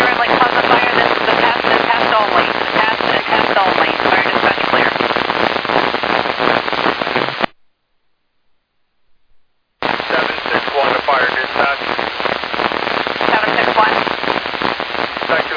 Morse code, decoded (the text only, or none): none